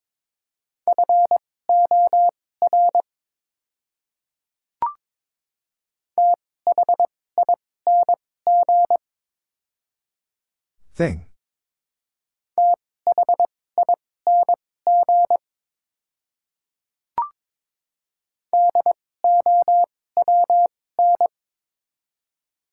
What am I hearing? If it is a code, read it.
Morse code, heard as a keyed tone, about 22 wpm, spaced as for 16 wpm, 700 Hz. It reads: FOR THING THING DOWN
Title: 2x Character Spacing - Top 100 Words - 22wpm